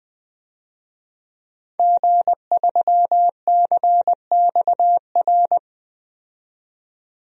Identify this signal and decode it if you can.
Morse code, heard as a keyed tone, about 20 wpm, 700 Hz. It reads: G3CXR
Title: DXCC-100 Call Signs - 5 Characters - 20wpm